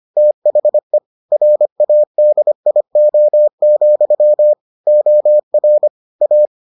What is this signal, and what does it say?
Morse code, heard as a keyed tone, about 25 wpm, 600 Hz. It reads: THE RADIO, OR A